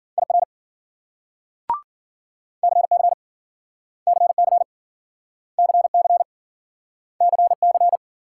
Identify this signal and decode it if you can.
Morse code, heard as a keyed tone, about 52 wpm, 700 Hz. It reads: IR CC CC CC CC